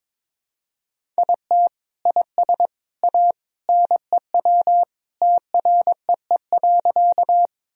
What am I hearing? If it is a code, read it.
Morse code, heard as a keyed tone, about 22 wpm, 700 Hz. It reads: IT IS A NEW TREE.
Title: Sentences from Top 1000 Words - Review Part 1 - 22wpm